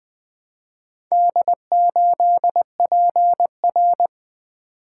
Morse code, heard as a keyed tone, about 20 wpm, 700 Hz. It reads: D8PR
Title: Four Letter-Number Combinations - 20wpm